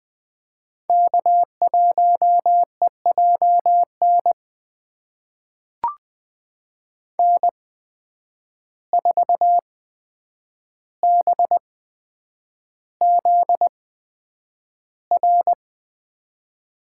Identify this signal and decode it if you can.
Morse code, heard as a keyed tone, about 20 wpm, spaced as for 9 wpm, 700 Hz. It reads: K1EJN N4BZR
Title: USA Call Signs - 5 Characters - 8x Character Spacing - 20wpm